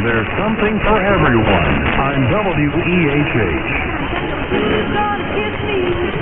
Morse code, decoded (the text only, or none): none